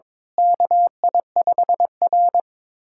Morse code, heard as a keyed tone, about 22 wpm, 700 Hz. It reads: KI5R
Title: USA Callsign DE Callsign - 4 Characters - 22wpm